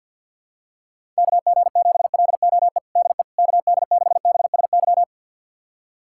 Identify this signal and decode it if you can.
Morse code, heard as a keyed tone, about 49 wpm, 700 Hz. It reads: KG7LOE DE KD6BSY
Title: USA Callsign DE Callsign - 6 Characters - 50wpm